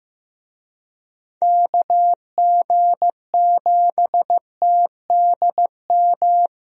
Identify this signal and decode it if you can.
Morse code, heard as a keyed tone, about 15 wpm, 700 Hz. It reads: KG7TDM
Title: USA Call Signs - 6 Characters - Repeated 2x - 15wpm